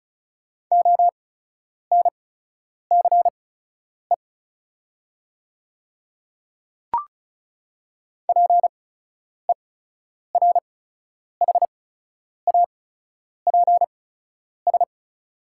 Morse code, heard as a keyed tone, about 36 wpm, spaced as for 10 wpm, 690 Hz. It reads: ONCE PERHAPS
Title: 8x Character Spacing - Top 500 Words - 35wpm